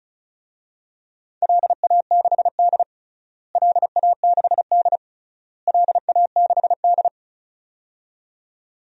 Morse code, heard as a keyed tone, about 36 wpm, 700 Hz. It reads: LA6D LA6D LA6D
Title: DXCC-100 Call Signs - 4 Characters - Repeated 3x - 35wpm